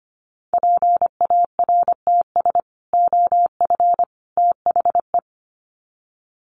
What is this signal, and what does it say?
Morse code, heard as a keyed tone, about 25 wpm, 700 Hz. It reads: PARTS OF THE
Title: Sets of 3 Words 25wpm